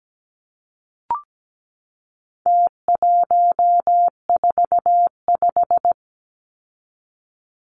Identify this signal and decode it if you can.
Morse code, heard as a keyed tone, about 17 wpm, 700 Hz. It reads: T145
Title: Numbers - Contest Style - 4-Digit 17wpm